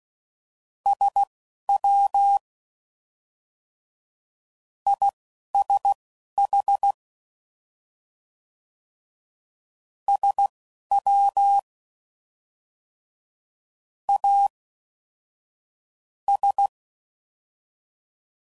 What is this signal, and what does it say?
Morse code, heard as a keyed tone, about 16 wpm, spaced as for 6 wpm, 800 Hz. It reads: SWISH SWAS